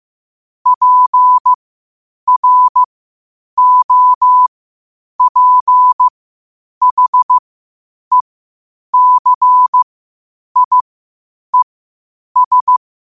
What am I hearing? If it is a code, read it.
Morse code, heard as a keyed tone, about 15 wpm, spaced as for 8 wpm, 1000 Hz. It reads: PROPHECIES
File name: qotdfiles / 8wpm